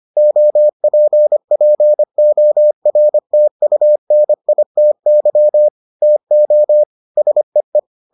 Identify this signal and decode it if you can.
Morse code, heard as a keyed tone, about 25 wpm, 600 Hz. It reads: OPPORTUNITY TO SEE